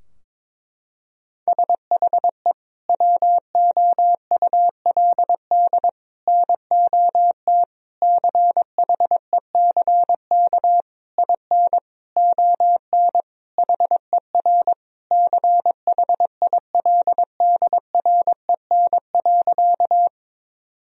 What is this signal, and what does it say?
Morse code, heard as a keyed tone, about 21 wpm, 700 Hz. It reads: SHE WOULD NOT CHECK IN ON HER CHILDREN.
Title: Sentences from Top 1000 Words - Review Part 3 - 22wpm